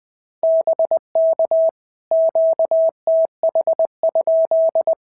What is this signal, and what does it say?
Morse code, heard as a keyed tone, about 20 wpm, 650 Hz. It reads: BK QTH?
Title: POTA QSO Practice - 20wpm